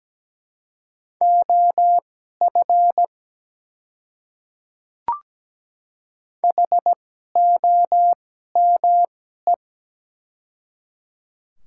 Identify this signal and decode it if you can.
Morse code, heard as a keyed tone, about 17 wpm, spaced as for 12 wpm, 700 Hz. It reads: OF HOME
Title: 2x Character Spacing - Top 300 Words - 17wpm